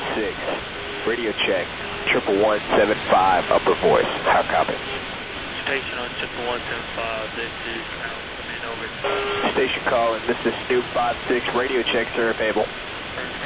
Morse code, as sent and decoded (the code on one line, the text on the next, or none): none